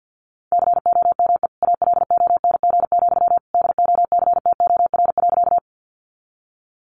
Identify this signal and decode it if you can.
Morse code, heard as a keyed tone, about 50 wpm, 700 Hz. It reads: COME ALONG, DOCTOR.
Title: Sentences from Top 1500 Words 50wpm